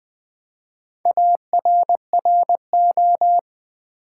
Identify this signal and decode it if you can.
Morse code, heard as a keyed tone, about 20 wpm, 700 Hz. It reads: ARRO
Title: Build A Word - 1 Letter At A Time - 5-Letter Words - 20wpm